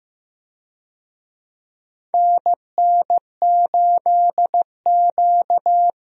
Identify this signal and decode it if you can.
Morse code, heard as a keyed tone, about 15 wpm, 700 Hz. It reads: NN8Q
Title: USA Call Signs - 4 Characters - Repeated 2x - 15wpm